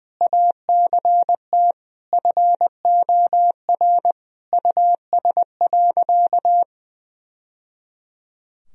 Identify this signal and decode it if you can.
Morse code, heard as a keyed tone, about 21 wpm, 700 Hz. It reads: ACT FOR US.
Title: Sentences from Top 1000 Words - Review Part 1 - 20wpm